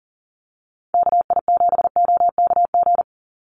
Morse code, heard as a keyed tone, about 40 wpm, 700 Hz. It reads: KI7OKG
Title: USA Call Signs 40wpm